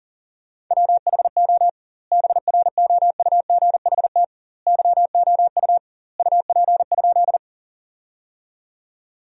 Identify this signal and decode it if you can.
Morse code, heard as a keyed tone, about 40 wpm, 700 Hz. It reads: WHO BROUGHT YOU UP?
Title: 2x Word-Spacing - Sentences from Top 500 Words - 40wpm